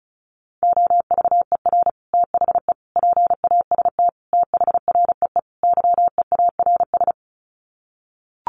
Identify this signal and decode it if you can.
Morse code, heard as a keyed tone, about 34 wpm, 700 Hz. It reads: OVER THE PAST THREE YEARS
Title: Sets of 5 Words Encore2 35wpm